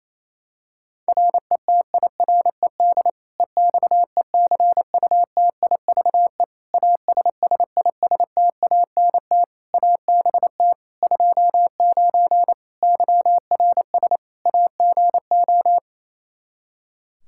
Morse code, heard as a keyed tone, about 28 wpm, 700 Hz. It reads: RETIRED EXECUTIVE ASSISTANT ABT 29 YRS AGO
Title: Practice QSO - Exchange 3 - More Info - 28wpm